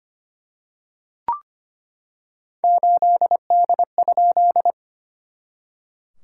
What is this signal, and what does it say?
Morse code, heard as a keyed tone, about 25 wpm, 700 Hz. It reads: 8D?
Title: SOTA QSO Practice 25wpm